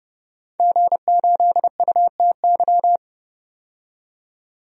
Morse code, heard as a keyed tone, about 32 wpm, 700 Hz. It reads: G8UTY
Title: DXCC-100 Call Signs - 5 Characters - 30wpm